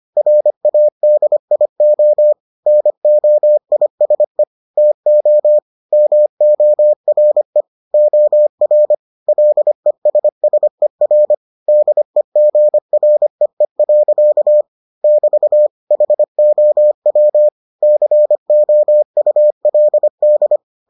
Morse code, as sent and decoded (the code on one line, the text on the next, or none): .-. .- -.. .. --- / -. --- .. ... / - --- / -- --- .-. / --- .-. / .-.. ... ... .-. / -.. --. .-. . .-.-.- / -...- / .... --- .-- / -.-. --- ..- .-.. -..
RADIO NOISE TO MORE OR LESSER DEGREE. <BT> HOW COULD